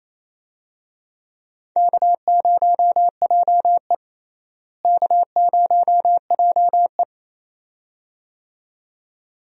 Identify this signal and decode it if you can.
Morse code, heard as a keyed tone, about 28 wpm, 700 Hz. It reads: K0JE K0JE
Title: USA Call Signs - 4 Characters - Repeated 2x - 28wpm